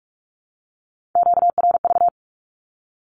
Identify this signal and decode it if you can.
Morse code, heard as a keyed tone, about 46 wpm, 700 Hz. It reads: QRV
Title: Q-Codes with Spoken Meaning - 45wpm